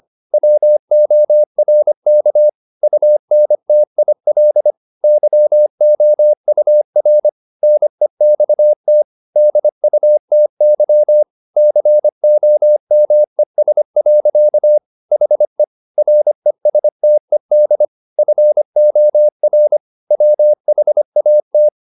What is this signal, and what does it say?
Morse code, heard as a keyed tone, about 25 wpm, 600 Hz. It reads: WORK UNTIL YOUR NEXT DUTY COMES. HE RESTED FOR WHAT